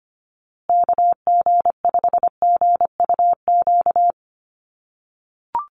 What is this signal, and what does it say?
Morse code, heard as a keyed tone, about 25 wpm, 700 Hz. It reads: KG5GUQ
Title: USA Call Signs Repeated 3x 25wpm